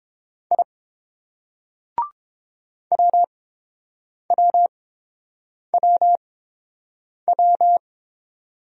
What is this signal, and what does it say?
Morse code, heard as a keyed tone, about 29 wpm, spaced as for 7 wpm, 690 Hz. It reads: I WWWW